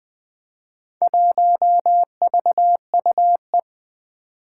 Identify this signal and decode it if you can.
Morse code, heard as a keyed tone, about 20 wpm, 700 Hz. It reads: 1VUE
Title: Four Challenging Letter-Number Combinations 20wpm